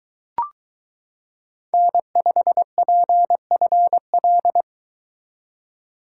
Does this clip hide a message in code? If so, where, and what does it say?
Yes: Morse code from 1.7 to 4.6 s, heard as a keyed tone, about 22 wpm, 700 Hz. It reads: N5PFL